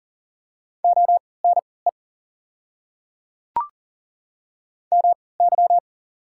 Morse code, heard as a keyed tone, about 40 wpm, spaced as for 23 wpm, 690 Hz. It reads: ONE MY
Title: 3x Character Spacing - Top 100 Words - 40wpm